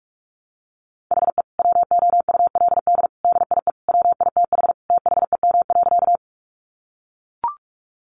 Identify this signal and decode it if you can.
Morse code, heard as a keyed tone, about 44 wpm, 700 Hz. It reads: HE WOULD DIE WITH THEM.
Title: Sentences from Top 1300 Words 45wpm